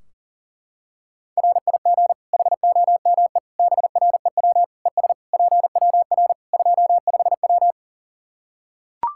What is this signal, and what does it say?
Morse code, heard as a keyed tone, about 40 wpm, 700 Hz. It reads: RIG HOME BREW ES PWR 25W
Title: Practice QSO - Exchange 2 - Info - 40wpm